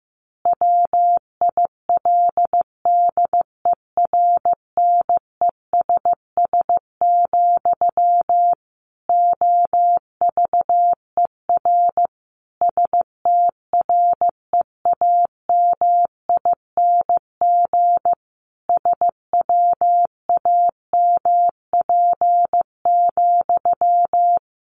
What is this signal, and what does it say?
Morse code, heard as a keyed tone, about 15 wpm, 700 Hz. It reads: WILDERNESS, OVER STREAMING SWAMP,
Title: A Sound Of Thunder 15wpm